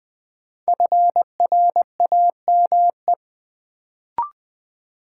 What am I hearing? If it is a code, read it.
Morse code, heard as a keyed tone, about 20 wpm, 700 Hz. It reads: FRAME
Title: Sentences from Top 1000 Words - Review Part 6 - New Word Drill - 20wpm